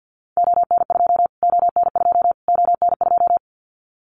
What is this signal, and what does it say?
Morse code, heard as a keyed tone, about 48 wpm, 700 Hz. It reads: ON2 ON2 ON2
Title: Callsign Prefixes for Top 25 DXCC - Repeated 3x - 50wpm